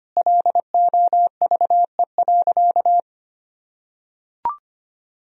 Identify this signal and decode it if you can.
Morse code, heard as a keyed tone, about 25 wpm, 700 Hz. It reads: LOVE.
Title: Sentences from Top 400 Words and Abbreviations 25wpm